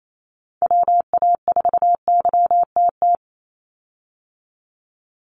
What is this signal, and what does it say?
Morse code, heard as a keyed tone, about 28 wpm, 700 Hz. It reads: WA4YTT
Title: USA Call Signs 28wpm